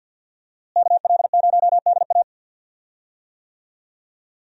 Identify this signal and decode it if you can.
Morse code, heard as a keyed tone, about 49 wpm, 700 Hz. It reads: KL0DA
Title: USA Call Signs - 5 Characters - Repeated 2x - 50wpm